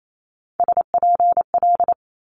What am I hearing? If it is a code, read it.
Morse code, heard as a keyed tone, about 28 wpm, 700 Hz. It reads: SPL
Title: Multi-letter Phonemes Quick Fire 28wpm